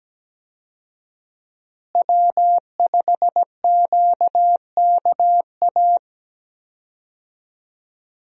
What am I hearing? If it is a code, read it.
Morse code, heard as a keyed tone, about 17 wpm, 700 Hz. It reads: W5QKA
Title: USA Call Signs - 5 Characters - Repeated 2x - 17wpm